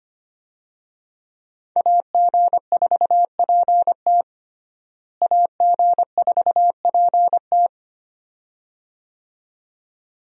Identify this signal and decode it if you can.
Morse code, heard as a keyed tone, about 25 wpm, 700 Hz. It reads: AG4PT AG4PT
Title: USA Call Signs - 5 Characters - Repeated 2x - 25wpm